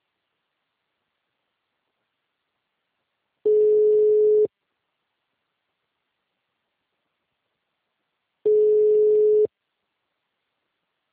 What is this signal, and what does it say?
Ringback tone (Europe, 425 Hz).